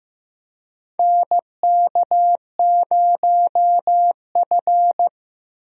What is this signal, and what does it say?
Morse code, heard as a keyed tone, about 16 wpm, 700 Hz. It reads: NK0F
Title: USA Call Signs - 4 Characters - Repeated 3x - 15wpm